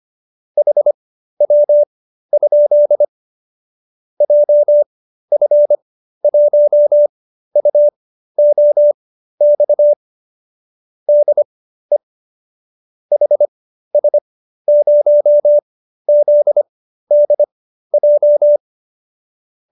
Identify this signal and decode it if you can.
Morse code, heard as a keyed tone, about 25 wpm, spaced as for 13 wpm, 590 Hz. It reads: HW? JF1UOX DE HS0ZDJ